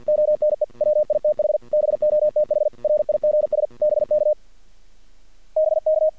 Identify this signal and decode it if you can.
Morse code, heard as a keyed tone, about 50 wpm, 620 Hz. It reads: ONE WITH BOTH RIGS BK BK
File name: archived / CRSnet